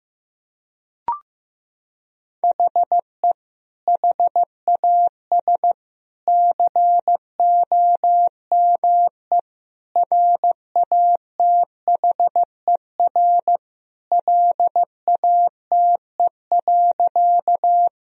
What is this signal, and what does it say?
Morse code, heard as a keyed tone, about 15 wpm, 700 Hz. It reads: HE HAS COME RATHER LATE.
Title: Sentences from Top 1000 Words - Review Part 4 - 15wpm